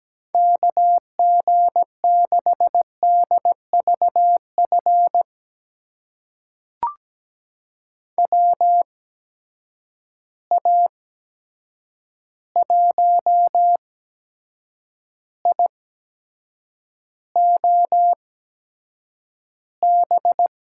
Morse code, heard as a keyed tone, about 17 wpm, spaced as for 7 wpm, 700 Hz. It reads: KG6DVF WA1IOB
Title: USA Call Signs - 6 Characters - 8x Character Spacing - 17wpm